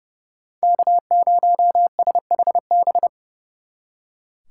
Morse code, heard as a keyed tone, about 40 wpm, 700 Hz. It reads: K0SHB